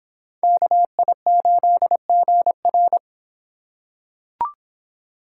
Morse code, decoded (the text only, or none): KI8GR